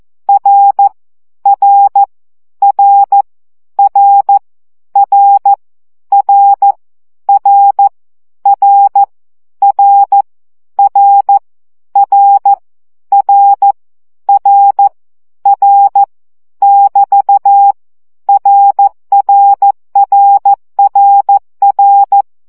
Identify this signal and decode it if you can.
Morse code, heard as a keyed tone, about 14 wpm, 800 Hz. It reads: R R R R R R R R R R R R R R <BT> RRRRR
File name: CW gyakorlas r 12 baud